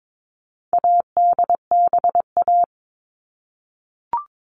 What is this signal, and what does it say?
Morse code, heard as a keyed tone, about 22 wpm, 700 Hz. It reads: ADBA
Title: All Cut Numbers - 4-Digit 22wpm